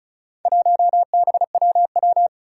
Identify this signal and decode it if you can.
Morse code, heard as a keyed tone, about 36 wpm, 700 Hz. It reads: 1BWW
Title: Four Challenging Letter-Number Combinations 35wpm